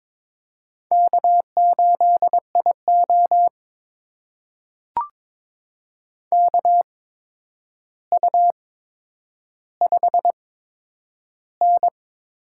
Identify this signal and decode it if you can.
Morse code, heard as a keyed tone, about 22 wpm, spaced as for 9 wpm, 700 Hz. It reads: K8IO KU5N
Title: USA Call Signs - 4 Characters - 8x Character Spacing - 22wpm